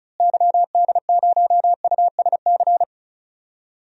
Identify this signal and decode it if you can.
Morse code, heard as a keyed tone, about 36 wpm, 700 Hz. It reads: YD0USC
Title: DXCC-100 Call Signs - 6 Characters - Repeated 2x - 35wpm